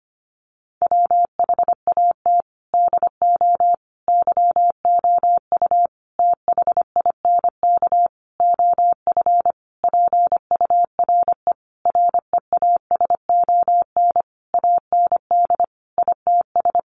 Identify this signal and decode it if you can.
Morse code, heard as a keyed tone, about 25 wpm, 700 Hz. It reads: WHAT DO YOU THINK OF PURE REASON AND ITS
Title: Sentences from Top 1500 Words 25wpm